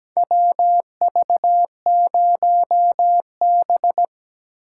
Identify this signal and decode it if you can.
Morse code, heard as a keyed tone, about 17 wpm, 700 Hz. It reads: WV0B
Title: USA Call Signs - 4 Characters - 4x Character Spacing - 17wpm